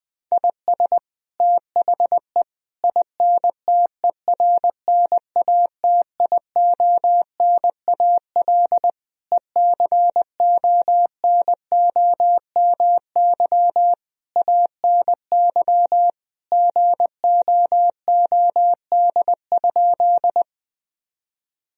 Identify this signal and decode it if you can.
Morse code, heard as a keyed tone, about 20 wpm, 700 Hz. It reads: IS THE INTERNATIONAL ECONOMY ANY GOOD?
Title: Sentences from Top 1000 Words - Review Part 4 - 20wpm